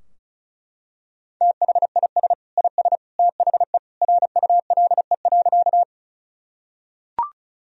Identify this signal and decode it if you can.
Morse code, heard as a keyed tone, about 36 wpm, 700 Hz. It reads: THIS IS THE RULE.
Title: Sentences from Top 1000 Words - Review Part 2 - 35wpm